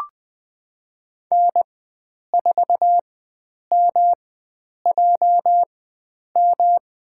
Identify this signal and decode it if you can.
Morse code, heard as a keyed tone, about 20 wpm, spaced as for 9 wpm, 700 Hz. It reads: N4MJM
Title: USA Call Signs - 5 Characters - 4x Character Spacing - 20wpm